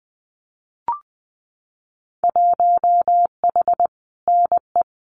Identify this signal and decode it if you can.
Morse code, heard as a keyed tone, about 20 wpm, 700 Hz. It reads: 1H NE
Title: Winter Field Day Exchanges 20wpm